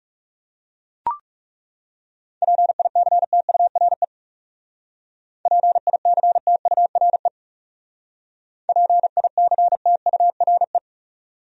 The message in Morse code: PICTURE PICTURE PICTURE

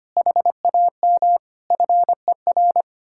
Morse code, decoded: HAM FER

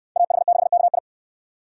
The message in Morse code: AH6KI

